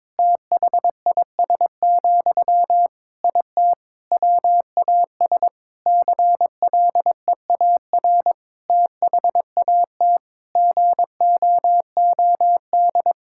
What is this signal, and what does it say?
Morse code, heard as a keyed tone, about 22 wpm, 700 Hz. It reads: THIS, IT WAS CLEAR THAT GOOD
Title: Sentences from Top 1000 Words - Review Part 6 - 22wpm